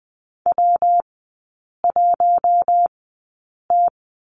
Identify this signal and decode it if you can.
Morse code, heard as a keyed tone, about 20 wpm, spaced as for 8 wpm, 700 Hz. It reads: W1T
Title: 152 Lesson 25e - Letter p - Three Characters 20wpm